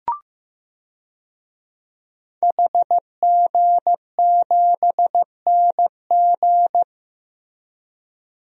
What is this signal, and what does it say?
Morse code, heard as a keyed tone, about 15 wpm, 700 Hz. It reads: HG7NG
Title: DXCC-100 Call Signs - 5 Characters - Repeated 2x - 15wpm